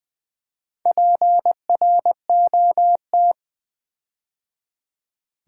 Morse code, heard as a keyed tone, about 20 wpm, 700 Hz. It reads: PROT
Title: Build A Word - 4 Letters At A Time - 8-Letter Words - 20wpm